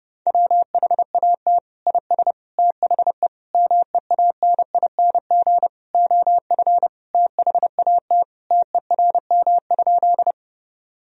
Morse code, heard as a keyed tone, about 30 wpm, 700 Hz. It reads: WHAT IS THE MEANING OF THAT TERM?
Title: Sentences from Top 1000 Words - Review Part 4 - 30wpm